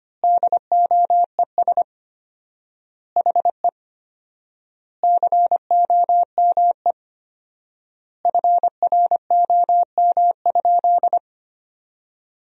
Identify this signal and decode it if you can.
Morse code, heard as a keyed tone, about 25 wpm, 700 Hz. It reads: DOES HE COME FROM?
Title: 4x Word-Spacing - Sentences from Top 300 Words - 25wpm